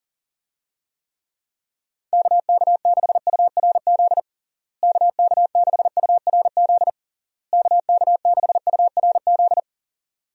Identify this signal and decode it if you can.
Morse code, heard as a keyed tone, about 40 wpm, 700 Hz. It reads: KK6URZ KK6URZ KK6URZ
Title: USA Call Signs - 6 Characters - Repeated 3x - 40wpm